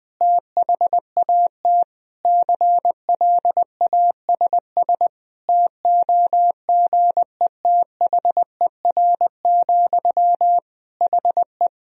Morse code, heard as a keyed tone, about 20 wpm, 700 Hz. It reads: THAT CLASS TOGETHER, HE
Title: Sentences from Top 1000 Words - Review Part 2 - 20wpm